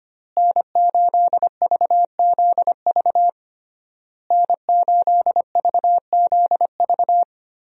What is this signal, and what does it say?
Morse code, heard as a keyed tone, about 25 wpm, 700 Hz. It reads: N8VZV N8VZV
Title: USA Call Signs - 5 Characters - Repeated 3x - 25wpm